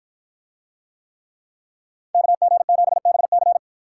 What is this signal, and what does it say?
Morse code, heard as a keyed tone, about 51 wpm, 700 Hz. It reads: KG7BC